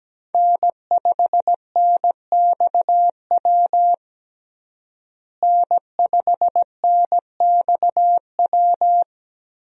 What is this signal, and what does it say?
Morse code, heard as a keyed tone, about 17 wpm, 700 Hz. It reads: N5NXW N5NXW
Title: USA Call Signs - 5 Characters - Repeated 2x - 17wpm